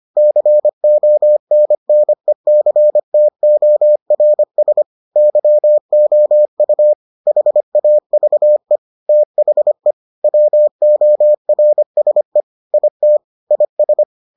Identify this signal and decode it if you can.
Morse code, heard as a keyed tone, about 25 wpm, 600 Hz. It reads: CONNECTORS YOU HAVE THE WORSE IT IS